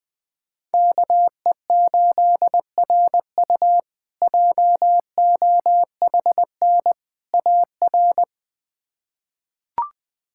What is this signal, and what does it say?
Morse code, heard as a keyed tone, about 20 wpm, 700 Hz. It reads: KE8RU JOHN AR
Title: K1USN SST QSO Practice - 20wpm